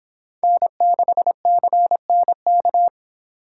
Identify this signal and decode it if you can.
Morse code, heard as a keyed tone, about 25 wpm, 700 Hz. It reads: N6CNK